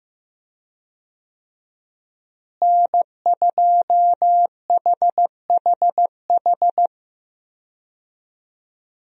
Morse code, heard as a keyed tone, about 15 wpm, 700 Hz. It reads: N2HHH
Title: USA Call Signs - 5 Characters - Repeated 2x - 15wpm